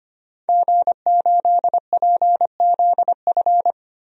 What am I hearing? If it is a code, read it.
Morse code, heard as a keyed tone, about 25 wpm, 700 Hz. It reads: G8PZF
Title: DXCC-100 Call Signs - 5 Characters - 25wpm